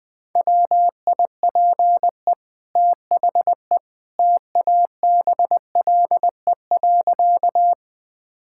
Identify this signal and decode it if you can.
Morse code, heard as a keyed tone, about 21 wpm, 700 Hz. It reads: WIPE THE TABLE.
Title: Easy Sentences with Hints - 20wpm